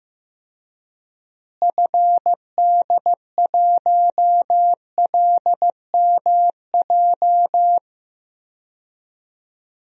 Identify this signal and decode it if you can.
Morse code, heard as a keyed tone, about 15 wpm, 700 Hz. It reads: FD1LMJ
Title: DXCC-100 Call Signs - 6 Characters - Repeated 2x - 15wpm